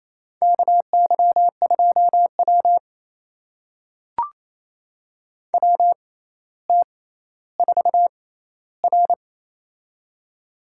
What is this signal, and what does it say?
Morse code, heard as a keyed tone, about 28 wpm, spaced as for 14 wpm, 700 Hz. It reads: KY2W WT4R
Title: USA Call Signs - 4 Characters - 6x Character Spacing - 28wpm